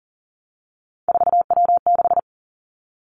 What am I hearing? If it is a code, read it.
Morse code, heard as a keyed tone, about 38 wpm, 700 Hz. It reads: 4W6
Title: Three Challenging Letter-Number Combinations 40wpm